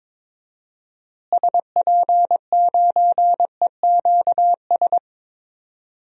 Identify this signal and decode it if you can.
Morse code, heard as a keyed tone, about 22 wpm, 700 Hz. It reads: SP9EQS
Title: DXCC-100 Call Signs - 6 Characters - Repeated 3x - 22wpm